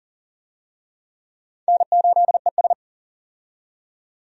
Morse code, heard as a keyed tone, about 40 wpm, 700 Hz. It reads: N8ES
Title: USA Call Signs - 4 Characters - 40wpm